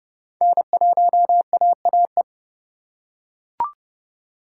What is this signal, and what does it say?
Morse code, heard as a keyed tone, about 29 wpm, 700 Hz. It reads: N1AAE